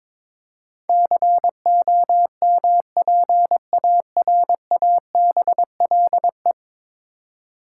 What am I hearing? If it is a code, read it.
Morse code, heard as a keyed tone, about 22 wpm, 700 Hz. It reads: COMPARABLE
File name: practice / 22wpm / 10 Letter Words - First 3 Letters Given - 22wpm